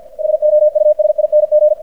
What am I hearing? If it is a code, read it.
Morse code, heard as a keyed tone, about 50 wpm, 600 Hz, with radio noise. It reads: SOUSING
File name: archived / CRSnet